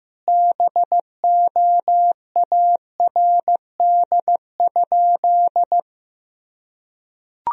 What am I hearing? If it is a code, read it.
Morse code, heard as a keyed tone, about 15 wpm, 700 Hz. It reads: BOARD?
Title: Sentences from Top 1000 Words - Review Part 4 - 15wpm